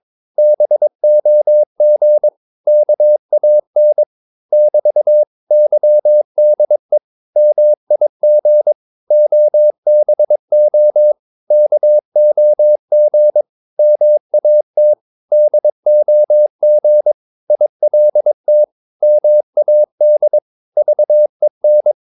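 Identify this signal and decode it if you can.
Morse code, heard as a keyed tone, about 23 wpm, 600 Hz. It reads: BOG KAN <BT> YDE MIG OBO KOG MAT DOG ILT MAD VEN
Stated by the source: Morse practice of short Danish words at 22 wpm, from CW med Gnister.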